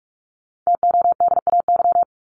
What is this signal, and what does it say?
Morse code, heard as a keyed tone, about 44 wpm, 700 Hz. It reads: TODAY
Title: Top 501-750 Words 45wpm